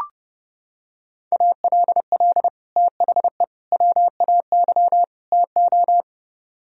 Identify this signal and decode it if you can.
Morse code, heard as a keyed tone, about 29 wpm, 700 Hz. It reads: ALL THE WAY TO